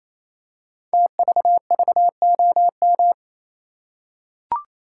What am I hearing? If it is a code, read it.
Morse code, heard as a keyed tone, about 27 wpm, 700 Hz. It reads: TVVOM
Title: Code Groups - 5 Letters - 28wpm